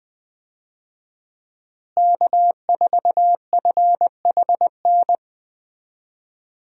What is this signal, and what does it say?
Morse code, heard as a keyed tone, about 20 wpm, 700 Hz. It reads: K4FHN
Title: USA Call Signs - 5 Characters - Repeated 2x - 20wpm